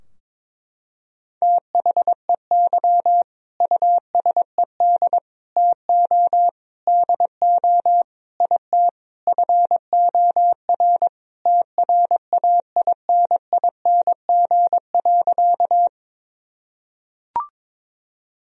Morse code, heard as a keyed tone, about 22 wpm, 700 Hz. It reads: THEY USED TO DO IT FOR TRAINING.
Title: Sentences from Top 1000 Words - Review Part 4 - 22wpm